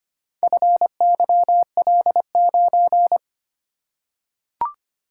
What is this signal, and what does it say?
Morse code, heard as a keyed tone, about 25 wpm, 700 Hz. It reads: FYL9
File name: practice / 25wpm / Four Letter-Number Combinations - 25wpm